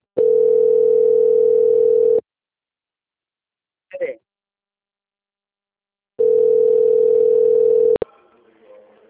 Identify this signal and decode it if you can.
Ringback tone (North America), 440 + 480 Hz, with speech.
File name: out- - - - - .14454